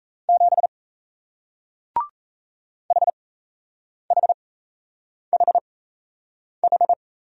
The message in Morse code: 7 HHHH

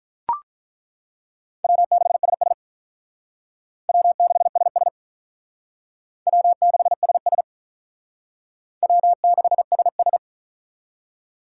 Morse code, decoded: W6SS W6SS W6SS W6SS

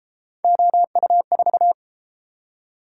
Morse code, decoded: OU4

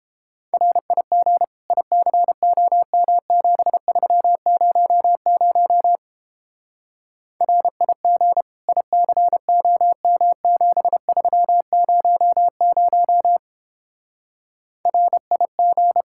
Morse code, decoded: RIG ICOM7300 RIG ICOM7300 RIG

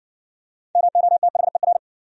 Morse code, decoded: MOTHER

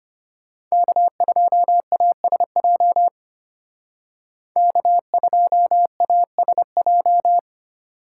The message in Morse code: K2ASJ K2ASJ

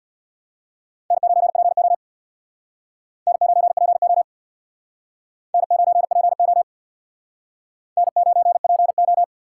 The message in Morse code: N9PO N9PO N9PO N9PO

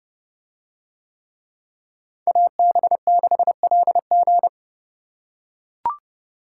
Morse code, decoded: AB6LG

